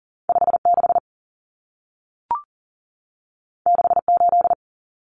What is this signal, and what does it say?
Morse code, heard as a keyed tone, about 40 wpm, 700 Hz. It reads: 56 68
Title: Numbers - 2-Digit 40wpm